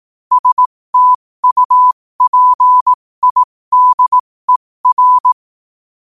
Morse code, heard as a keyed tone, about 18 wpm, spaced as for 15 wpm, 1000 Hz. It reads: STUPIDER
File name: qotdfiles / 15wpm